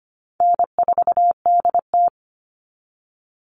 Morse code, heard as a keyed tone, about 25 wpm, 700 Hz. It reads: N4DT
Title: Easy Call Signs - 25wpm